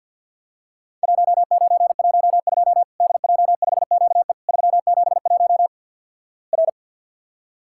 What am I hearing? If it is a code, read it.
Morse code, heard as a keyed tone, about 50 wpm, 700 Hz. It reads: 1912 DJ5QE 371 R
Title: Worked All Europe DX Contest - QTC Exchange - 50wpm